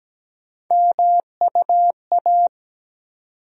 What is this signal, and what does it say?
Morse code, heard as a keyed tone, about 17 wpm, 700 Hz. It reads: MUA
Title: USA Call Sign Postfixes - 3 Letters 17wpm